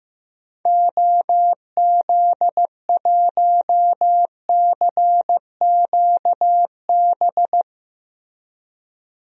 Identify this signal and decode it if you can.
Morse code, heard as a keyed tone, about 15 wpm, 700 Hz. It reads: OZ1CQB
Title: DXCC-100 Call Signs - 6 Characters - Repeated 3x - 15wpm